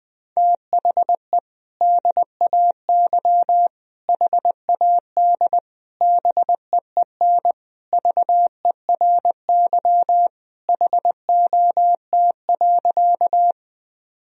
Morse code, heard as a keyed tone, about 20 wpm, 700 Hz. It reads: THE DAY HAD BEEN VERY HOT.
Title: Sentences from Top 1000 Words - Review Part 3 - 20wpm